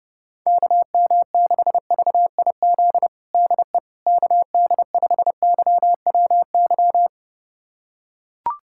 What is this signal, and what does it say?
Morse code, heard as a keyed tone, about 30 wpm, 700 Hz. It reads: KM6VIZ DE KD5YWY
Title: USA Callsign DE Callsign - 6 Characters - 30wpm